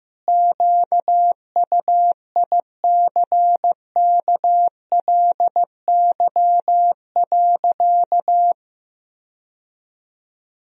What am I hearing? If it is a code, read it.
Morse code, heard as a keyed tone, about 15 wpm, 700 Hz. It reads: QUICKLY.
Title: Sentences from Top 1000 Words - Review Part 3 - 15wpm